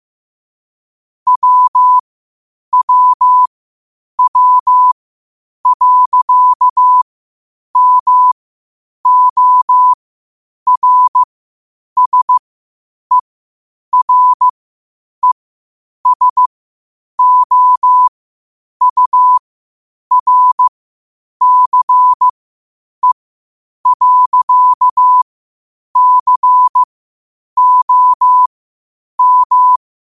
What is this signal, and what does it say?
Morse code, heard as a keyed tone, about 15 wpm, spaced as for 8 wpm, 1000 Hz. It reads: WWW.MORSERESOURCE.COM
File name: qotdfiles / 8wpm